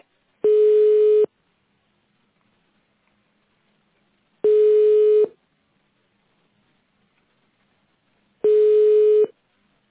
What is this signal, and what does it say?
Ringback tone (425 Hz, 1 s on, 3 s off).